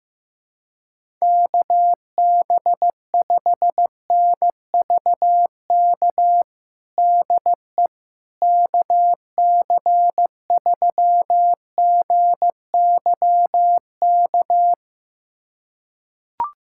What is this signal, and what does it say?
Morse code, heard as a keyed tone, about 15 wpm, 700 Hz. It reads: KB5NVK DE KC3GYK
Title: USA Callsign DE Callsign 15wpm